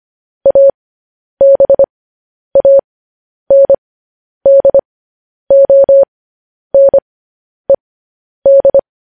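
Morse code, heard as a keyed tone, about 25 wpm, spaced as for 10 wpm, 550 Hz. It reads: ABANDONED
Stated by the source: Morse code news headlines practice transmission.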